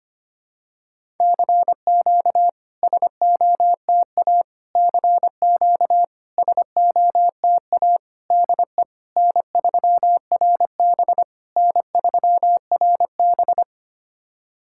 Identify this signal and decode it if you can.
Morse code, heard as a keyed tone, about 25 wpm, 690 Hz. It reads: CQ SOTA CQ SOTA DE N3RB N3RB
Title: SOTA QSO Practice 25wpm